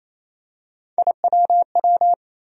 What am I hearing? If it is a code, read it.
Morse code, heard as a keyed tone, about 27 wpm, 700 Hz. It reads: IWW
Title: USA Call Sign Postfixes - 3 Letters 28wpm